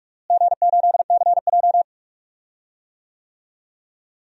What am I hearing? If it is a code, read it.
Morse code, heard as a keyed tone, about 44 wpm, 700 Hz. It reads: G8CJ
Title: DXCC-100 Call Signs - 4 Characters - Repeated 3x - 45wpm